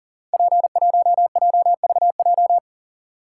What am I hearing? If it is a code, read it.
Morse code, heard as a keyed tone, about 38 wpm, 700 Hz. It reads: P1JVJ
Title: Code Groups - 5 Letter:Numbers - 40wpm